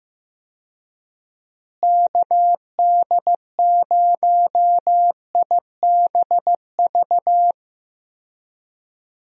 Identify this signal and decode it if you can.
Morse code, heard as a keyed tone, about 15 wpm, 700 Hz. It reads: KD0IBV
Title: USA Call Signs - 6 Characters - Repeated 2x - 15wpm